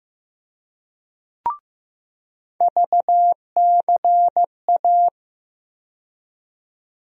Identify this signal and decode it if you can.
Morse code, heard as a keyed tone, about 15 wpm, 700 Hz. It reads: VCA
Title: USA Call Sign Postfixes - 3 Letters 15wpm